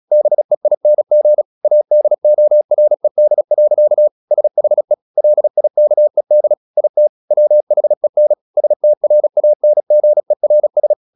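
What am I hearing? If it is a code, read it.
Morse code, heard as a keyed tone, about 36 wpm, 600 Hz. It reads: BEING ADORED. SHE LIKED IT WHEN STRANGERS
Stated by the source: Morse practice at 36 wpm, text of Ken Follett's novel Never, from CW med Gnister.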